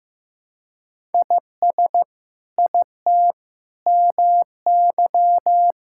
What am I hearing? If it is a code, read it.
Morse code, heard as a keyed tone, about 15 wpm, 700 Hz. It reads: IS IT MY
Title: Sentences from Top 1000 Words - Review Part 5 - 15wpm